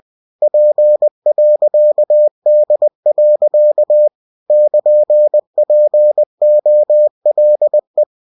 Morse code, heard as a keyed tone, about 20 wpm, 600 Hz. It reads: P.D. <KN>POLE
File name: cw-books / winnie-the-pooh / winnie-20wpm-page-0099